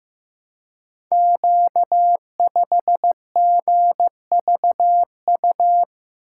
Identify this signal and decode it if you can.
Morse code, heard as a keyed tone, about 15 wpm, 700 Hz. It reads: Q5GVU
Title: Code Groups - 5 Letter:Numbers - 15wpm